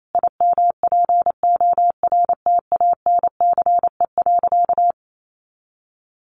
Morse code, heard as a keyed tone, about 28 wpm, 700 Hz. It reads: IMPORTANCE.
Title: Sentences from Top 1200 Words 28wpm